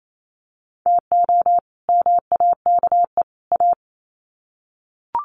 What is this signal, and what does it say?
Morse code, heard as a keyed tone, about 28 wpm, 700 Hz. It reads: TO MAKE A
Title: Sets of 3 Words 28wpm